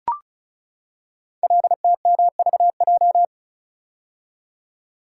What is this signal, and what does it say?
Morse code, heard as a keyed tone, about 36 wpm, 690 Hz. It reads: LTMVJ
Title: Code Groups - 5 Letters - 35wpm